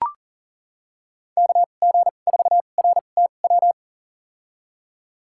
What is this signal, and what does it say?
Morse code, heard as a keyed tone, about 40 wpm, spaced as for 29 wpm, 700 Hz. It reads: KG4RTW
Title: USA Call Signs - 6 Characters - 2x Character Spacing - 40wpm